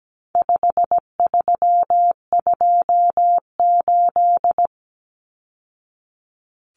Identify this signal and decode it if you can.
Morse code, heard as a keyed tone, about 17 wpm, 700 Hz. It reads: 5328
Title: Numbers - 4-Digit 17wpm